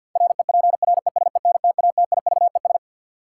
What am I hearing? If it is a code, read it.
Morse code, heard as a keyed tone, about 53 wpm, 700 Hz. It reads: REPRESENTATIVES
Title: Sets of 5 Words Encore - New Word Drill - 50wpm